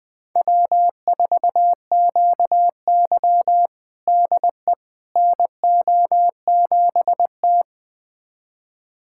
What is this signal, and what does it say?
Morse code, heard as a keyed tone, about 21 wpm, 700 Hz. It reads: W4QY DE NO7T
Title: USA Callsign DE Callsign - 4 Characters - 20wpm